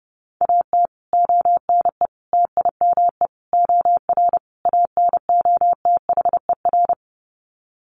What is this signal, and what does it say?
Morse code, heard as a keyed tone, about 30 wpm, 700 Hz. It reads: AT ONE TIME OR ANOTHER
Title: Sets of 5 Words Encore 30wpm